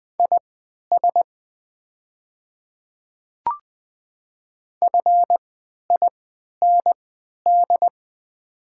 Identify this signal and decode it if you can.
Morse code, heard as a keyed tone, about 20 wpm, spaced as for 11 wpm, 690 Hz. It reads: IS FIND
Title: 3x Character Spacing - Top 100 Words - 20wpm